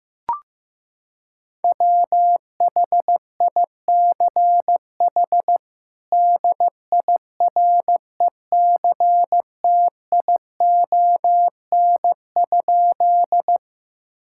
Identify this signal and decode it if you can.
Morse code, heard as a keyed tone, about 16 wpm, 700 Hz. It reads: WHICH DIRECTION?
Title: Sentences from Top 1000 Words - Review Part 5 - 15wpm